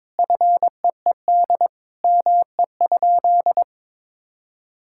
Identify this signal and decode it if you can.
Morse code, heard as a keyed tone, about 22 wpm, 700 Hz. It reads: FEED ME?
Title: Sentences from Top 1600 Words 22wpm